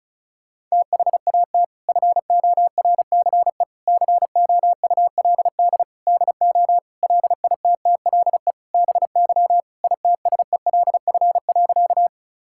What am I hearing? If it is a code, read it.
Morse code, heard as a keyed tone, about 36 wpm, 700 Hz. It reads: THAT FORCE COULD DO LITTLE BY ITSELF.
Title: Sentences from Top 1000 Words - Review Part 5 - 35wpm